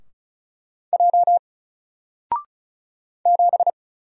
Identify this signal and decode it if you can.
Morse code, heard as a keyed tone, about 33 wpm, 700 Hz. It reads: J 7
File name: Single-Letter-Number-35wpm